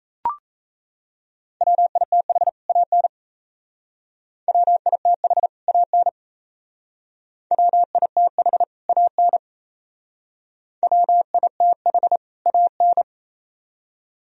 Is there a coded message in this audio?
Yes: Morse code from 1.6 to 13.0 s, heard as a keyed tone, about 35 wpm, 700 Hz. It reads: WITH AN WITH AN WITH AN WITH AN